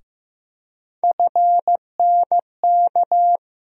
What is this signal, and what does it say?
Morse code, heard as a keyed tone, about 15 wpm, 700 Hz. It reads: FNK
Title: USA Call Sign Postfixes - 3 Letters 15wpm